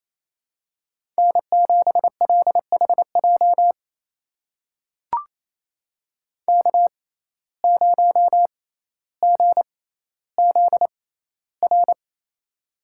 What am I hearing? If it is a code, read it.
Morse code, heard as a keyed tone, about 28 wpm, spaced as for 14 wpm, 700 Hz. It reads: N7LHJ K0GZR
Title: USA Call Signs - 5 Characters - 6x Character Spacing - 28wpm